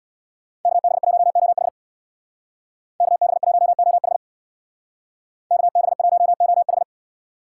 Morse code, heard as a keyed tone, about 62 wpm, 700 Hz. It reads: KB1OH KB1OH KB1OH
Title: Speed Racing - USA 2x2 Call Signs 50wpm